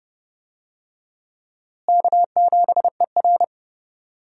Morse code, heard as a keyed tone, about 30 wpm, 700 Hz. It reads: K7ER
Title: USA Call Signs - 4 Characters - Repeated 2x - 30wpm